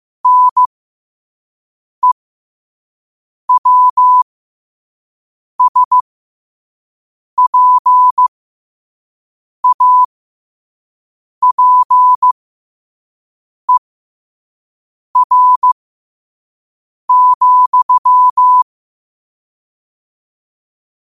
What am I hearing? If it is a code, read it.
Morse code, heard as a keyed tone, about 15 wpm, spaced as for 5 wpm, 1000 Hz. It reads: NEWSPAPER,
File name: qotdfiles / 5wpm